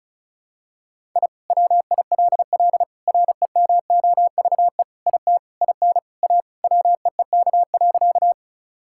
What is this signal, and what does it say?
Morse code, heard as a keyed tone, about 34 wpm, 700 Hz. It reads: I WILL REMOVE IT IN A WEEK.
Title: Sentences from Top 1000 Words - Review Part 5 - 35wpm